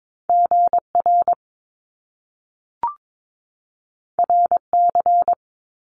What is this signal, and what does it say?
Morse code, heard as a keyed tone, about 22 wpm, 700 Hz. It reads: GR RC
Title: Top Two Letter Combinations in Words 22wpm